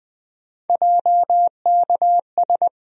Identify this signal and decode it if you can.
Morse code, heard as a keyed tone, about 20 wpm, 700 Hz. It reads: JKS
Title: USA Call Sign Postfixes - 3 Letters 20wpm